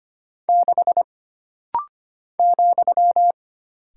Morse code, heard as a keyed tone, about 25 wpm, 700 Hz. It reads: 6 ,
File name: Single Character (Letter:Number:Punctuation:Prosign) - Rapid-Fire - 25wpm